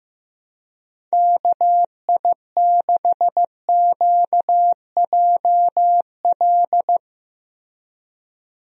Morse code, heard as a keyed tone, about 15 wpm, 700 Hz. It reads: KI6QJL
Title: USA Call Signs - 6 Characters - Repeated 2x - 15wpm